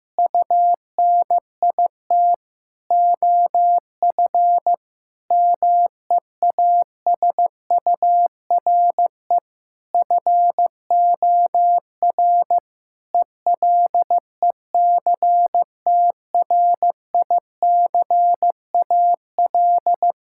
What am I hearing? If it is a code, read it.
Morse code, heard as a keyed tone, about 15 wpm, 700 Hz. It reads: UNIT OF MEASURE FOR ELECTRICAL
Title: Sentences from Top 1000 Words - Review Part 6 - 15wpm